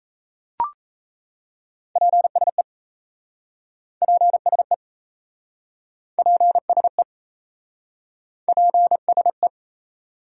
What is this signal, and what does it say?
Morse code, heard as a keyed tone, about 37 wpm, 700 Hz. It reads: PSE PSE PSE PSE